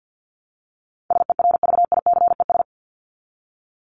Morse code, heard as a keyed tone, about 50 wpm, 700 Hz. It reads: SERVICES
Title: ES:S Suffix - Top 100 Words with Base Words 50wpm